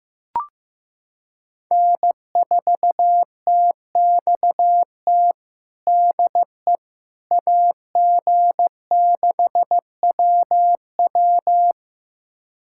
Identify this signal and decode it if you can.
Morse code, heard as a keyed tone, about 15 wpm, 700 Hz. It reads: N4TXT DE AG6WW
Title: USA Callsign DE Callsign - 5 Characters - 15wpm